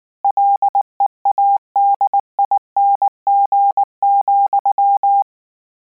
Morse code, heard as a keyed tone, about 19 wpm, 800 Hz. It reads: LEADING,
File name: Combinations / Combinations-mp3s / Combinations@18